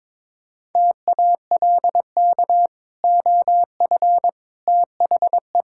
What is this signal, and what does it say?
Morse code, heard as a keyed tone, about 22 wpm, 700 Hz. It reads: TALK OF THE
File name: practice / 22wpm / Build A Phrase - 5-Word Phrases - 22wpm